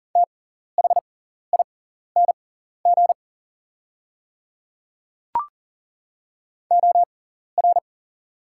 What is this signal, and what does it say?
Morse code, heard as a keyed tone, about 40 wpm, spaced as for 14 wpm, 690 Hz. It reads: THING OR
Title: 6x Character Spacing - Top 100 Words - 40wpm